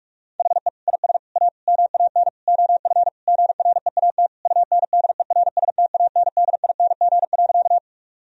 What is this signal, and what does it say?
Morse code, heard as a keyed tone, about 46 wpm, 700 Hz. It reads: HE IS A MAN OF GREAT UNDERSTANDING.
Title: Sentences from Top 1000 Words - Review Part 3 - 45wpm